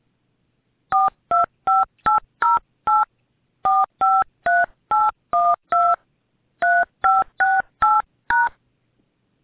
DTMF, heard as telephone touch tones, tones of about 200 ms. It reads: 425808 453812 3568#